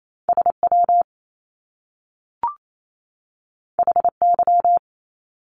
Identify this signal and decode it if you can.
Morse code, heard as a keyed tone, about 28 wpm, 700 Hz. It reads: SW HY
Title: Top Two Letter Combinations in Words 28wpm